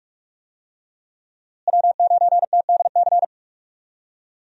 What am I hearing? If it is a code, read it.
Morse code, heard as a keyed tone, about 48 wpm, 700 Hz. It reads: W9TDC
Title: USA Call Signs - 5 Characters - Repeated 2x - 45wpm